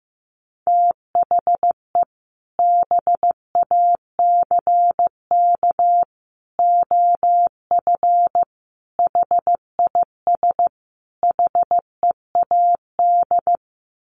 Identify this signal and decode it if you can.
Morse code, heard as a keyed tone, about 15 wpm, 700 Hz. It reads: THE BACK OF HIS HEAD
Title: Sets of 5 Words Encore 15wpm